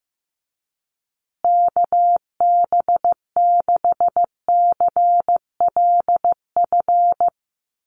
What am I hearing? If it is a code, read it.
Morse code, heard as a keyed tone, about 14 wpm, 700 Hz. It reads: KB6CLF
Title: USA Call Signs Repeated 3x 15wpm